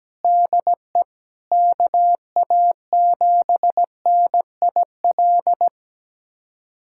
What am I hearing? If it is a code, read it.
Morse code, heard as a keyed tone, about 17 wpm, 700 Hz. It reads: DE KA7NIL
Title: USA Callsign DE Callsign 17wpm